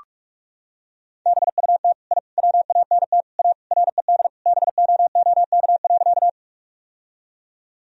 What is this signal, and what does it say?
Morse code, heard as a keyed tone, about 41 wpm, 700 Hz. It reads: BUT I WANT A RED BOOK.
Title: Sentences from Top 1000 Words - Review Part 2 - 45wpm